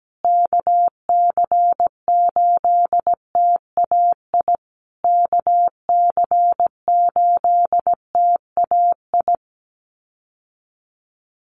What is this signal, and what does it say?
Morse code, heard as a keyed tone, about 17 wpm, 700 Hz. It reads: KC8TAI KC8TAI
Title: USA Call Signs Repeated 3x 17wpm